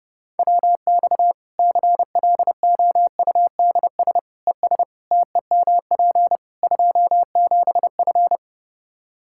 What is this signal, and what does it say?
Morse code, heard as a keyed tone, about 30 wpm, 700 Hz. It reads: WX CLOUDS ES TEMP 27F
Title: Practice QSO - Exchange 2 - Info - 30wpm